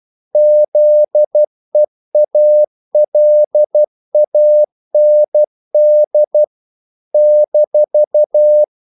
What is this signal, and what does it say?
Morse code, heard as a keyed tone, about 12 wpm, 600 Hz. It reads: ZEALAND -
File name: CW / Practice / NationalCapitals12